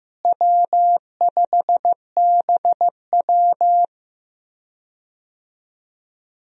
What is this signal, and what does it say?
Morse code, heard as a keyed tone, about 15 wpm, 700 Hz. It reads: W5BW
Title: USA Call Signs - 4 Characters - Repeated 2x - 15wpm